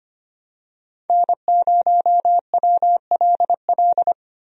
Morse code, heard as a keyed tone, about 25 wpm, 700 Hz. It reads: N0WLL
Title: USA Call Signs - 5 Characters - 25wpm